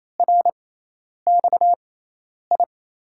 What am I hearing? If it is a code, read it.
Morse code, heard as a keyed tone, about 28 wpm, spaced as for 10 wpm, 700 Hz. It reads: RXI
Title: USA Call Signs - 6 Characters - 6x Character Spacing - 28wpm